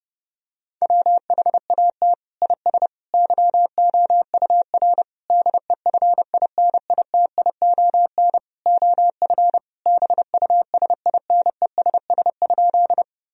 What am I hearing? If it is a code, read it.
Morse code, heard as a keyed tone, about 30 wpm, 700 Hz. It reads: WHAT IS YOUR DEFINITION OF BUSINESS?
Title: Sentences from Top 1000 Words - Review Part 4 - 30wpm